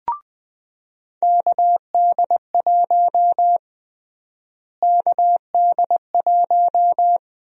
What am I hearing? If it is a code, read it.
Morse code, heard as a keyed tone, about 20 wpm, 700 Hz. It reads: KD1 KD1
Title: USA Call Sign Prefixes - Repeated 2x - 20wpm